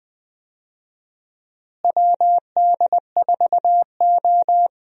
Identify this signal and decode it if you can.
Morse code, heard as a keyed tone, about 19 wpm, 700 Hz. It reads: WD4O
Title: USA Call Signs - 4 Characters - Repeated 3x - 20wpm